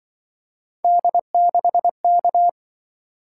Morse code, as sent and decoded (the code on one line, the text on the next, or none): -.. -.... -.-
D6K